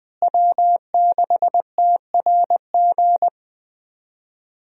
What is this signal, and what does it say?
Morse code, heard as a keyed tone, about 20 wpm, 700 Hz. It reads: W6TRG
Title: USA Call Signs - 5 Characters - Repeated 2x - 20wpm